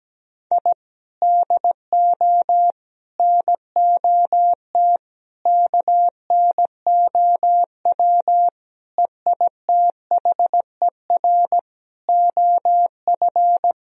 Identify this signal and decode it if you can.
Morse code, heard as a keyed tone, about 17 wpm, 700 Hz. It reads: I DO NOT KNOW EITHER OF
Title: Sentences from Top 1000 Words - Review Part 4 - 17wpm